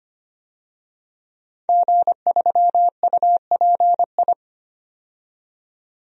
Morse code, heard as a keyed tone, about 25 wpm, 700 Hz. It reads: G3UPI
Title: DXCC-100 Call Signs - 5 Characters - 25wpm